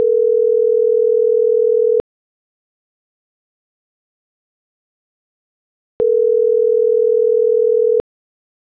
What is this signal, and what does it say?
Ringback tone (North America), 440 + 480 Hz.
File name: CENTRAL NMP / out melanie.diaznmp - .36734